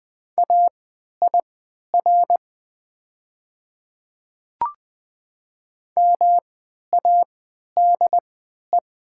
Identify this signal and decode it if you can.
Morse code, heard as a keyed tone, about 20 wpm, spaced as for 11 wpm, 700 Hz. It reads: AIR MADE
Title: 3x Character Spacing - Top 300 Words - 20wpm